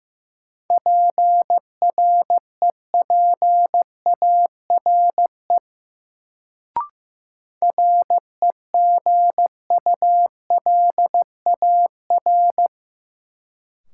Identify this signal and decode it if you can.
Morse code, heard as a keyed tone, about 16 wpm, 700 Hz. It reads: PREPARE REGULAR
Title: Sentences from Top 900 Words - New Word Drill - 15wpm